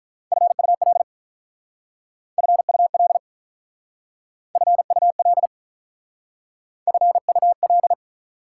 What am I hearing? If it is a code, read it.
Morse code, heard as a keyed tone, about 44 wpm, 700 Hz. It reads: FUL FUL FUL FUL